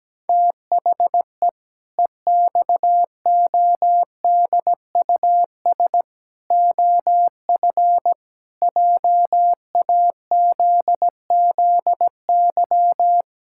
Morse code, heard as a keyed tone, about 17 wpm, 700 Hz. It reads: THE EXODUS OF JAZZY
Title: Pangram Challenge v2 - 17wpm